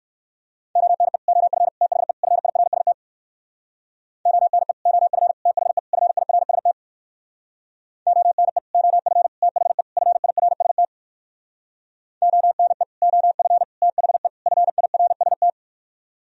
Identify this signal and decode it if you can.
Morse code, heard as a keyed tone, about 58 wpm, 700 Hz. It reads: ONE OF THE FIRST ONE OF THE FIRST ONE OF THE FIRST ONE OF THE FIRST